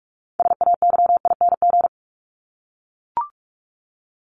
Morse code, heard as a keyed tone, about 44 wpm, 700 Hz. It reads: SAYING
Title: ING Suffix - Top 100 Words with Base Words 45wpm